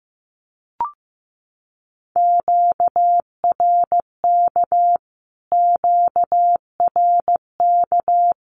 Morse code, heard as a keyed tone, about 14 wpm, 700 Hz. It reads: QRK QRK
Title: Q-Codes Repeated 3x 15wpm